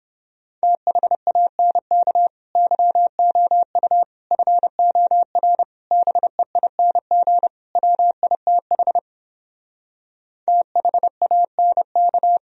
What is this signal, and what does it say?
Morse code, heard as a keyed tone, about 31 wpm, 700 Hz. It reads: THANK YOU FOR BEING WITH THANK